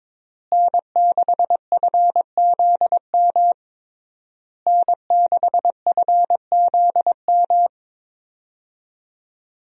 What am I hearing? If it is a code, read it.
Morse code, heard as a keyed tone, about 22 wpm, 700 Hz. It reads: N6FZM N6FZM
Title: USA Call Signs - 5 Characters - Repeated 3x - 22wpm